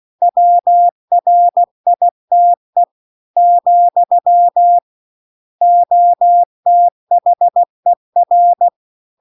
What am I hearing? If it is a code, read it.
Morse code, heard as a keyed tone, about 16 wpm, 700 Hz. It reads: WRITE , OTHER